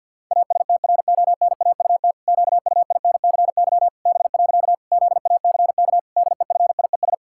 Morse code, heard as a keyed tone, about 51 wpm, 700 Hz. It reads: ASTRONAUT QUINCY B. ZACK DEFIES
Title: Pangram Challenge v2 - 50wpm